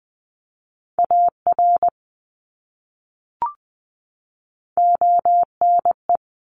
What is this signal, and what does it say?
Morse code, heard as a keyed tone, about 20 wpm, 700 Hz. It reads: AR ONE
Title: ARRL Sections 20wpm